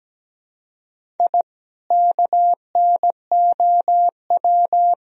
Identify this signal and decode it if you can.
Morse code, heard as a keyed tone, about 17 wpm, 700 Hz. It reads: I KNOW
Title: Sentences from Top 1000 Words - Review Part 3 - 17wpm